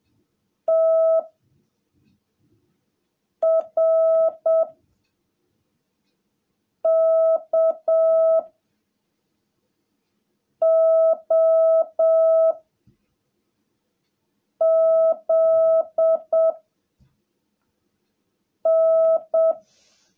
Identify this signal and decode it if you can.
Morse code, heard as a keyed tone, about 7 wpm, 660 Hz. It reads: T R K O Z N
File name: jh8cbh / 2021cw / cw17a